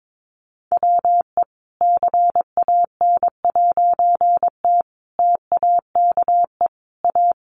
Morse code, heard as a keyed tone, about 22 wpm, 700 Hz. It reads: WE CAN'T TAKE A
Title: A Sound Of Thunder 22wpm